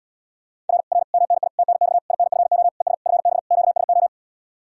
Morse code, heard as a keyed tone, about 80 wpm, 700 Hz. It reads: R R MNI TNX INFO ES FB QSO